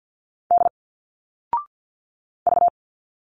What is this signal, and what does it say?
Morse code, heard as a keyed tone, about 48 wpm, 700 Hz. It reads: D V
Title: Single Letters 50wpm